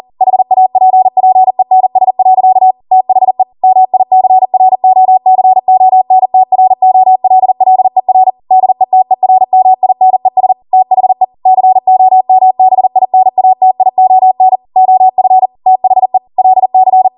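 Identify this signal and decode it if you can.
Morse code, heard as a keyed tone, about 40 wpm, 750 Hz. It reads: HAPPENS. THE MICROCONTROLLER DETERMINES THE COMBINATION OF THE LC
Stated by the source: Code practice at 40 wpm.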